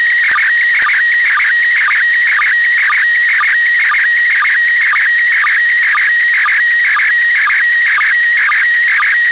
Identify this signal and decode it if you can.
SSTV (slow-scan television), PD 120.